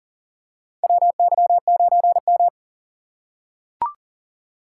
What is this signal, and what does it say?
Morse code, heard as a keyed tone, about 37 wpm, 700 Hz. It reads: WY9M
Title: USA Call Signs - 3x Character Spacing - 40wpm